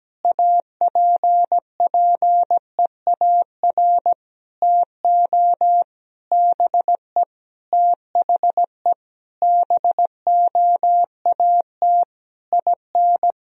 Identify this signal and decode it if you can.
Morse code, heard as a keyed tone, about 17 wpm, 700 Hz. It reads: APPEAR TO BE THE BOAT IN
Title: Sentences from Top 1000 Words - Review Part 3 - 17wpm